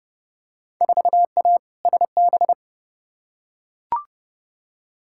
Morse code, heard as a keyed tone, about 30 wpm, 700 Hz. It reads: 4A SB
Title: Field Day Exchanges Encore - 30wpm